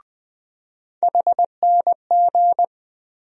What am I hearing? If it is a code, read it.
Morse code, heard as a keyed tone, about 20 wpm, 700 Hz. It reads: HNG
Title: USA Call Sign Postfixes - 3 Letters 20wpm